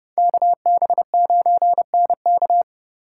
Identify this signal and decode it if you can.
Morse code, heard as a keyed tone, about 30 wpm, 700 Hz. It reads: KB9NK